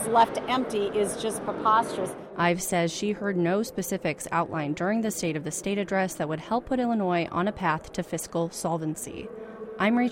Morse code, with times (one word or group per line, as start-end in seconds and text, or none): none